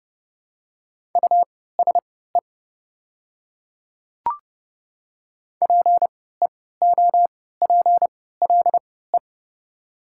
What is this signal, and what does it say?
Morse code, heard as a keyed tone, about 31 wpm, spaced as for 17 wpm, 700 Hz. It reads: USE PEOPLE
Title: 3x Character Spacing - Top 100 Words - 30wpm